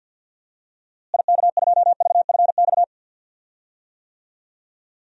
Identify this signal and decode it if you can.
Morse code, heard as a keyed tone, about 52 wpm, 700 Hz. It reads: IK2VFX
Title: DXCC-100 Call Signs - 6 Characters - Repeated 2x - 50wpm